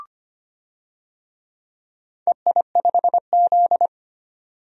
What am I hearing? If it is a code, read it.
Morse code, heard as a keyed tone, about 25 wpm, 700 Hz. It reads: EI5Z
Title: DXCC-100 Call Signs - 4 Characters - 25wpm